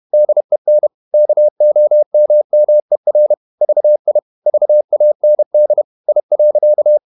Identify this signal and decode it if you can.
Morse code, heard as a keyed tone, about 31 wpm, 600 Hz. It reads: DEN KOMMER VI VAND I.
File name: Ferie QRQ 31wpm